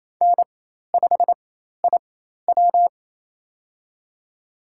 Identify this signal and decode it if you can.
Morse code, heard as a keyed tone, about 28 wpm, spaced as for 13 wpm, 700 Hz. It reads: N5IW
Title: USA Call Signs - 4 Characters - 4x Character Spacing - 28wpm